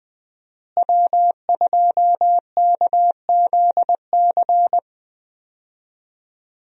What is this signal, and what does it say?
Morse code, heard as a keyed tone, about 20 wpm, 700 Hz. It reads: W2KZC
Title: USA Call Signs - 5 Characters - Repeated 3x - 20wpm